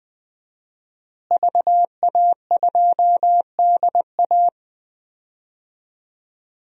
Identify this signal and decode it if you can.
Morse code, heard as a keyed tone, about 20 wpm, 700 Hz. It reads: VA2DA
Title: DXCC-100 Call Signs - 5 Characters - Repeated 3x - 20wpm